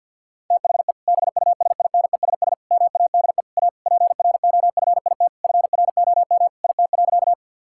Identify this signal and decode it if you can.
Morse code, heard as a keyed tone, about 50 wpm, 700 Hz. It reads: THE BUSINESS MADE A PROFIT FROM IT.
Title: Sentences from Top 1000 Words - Review Part 5 - 50wpm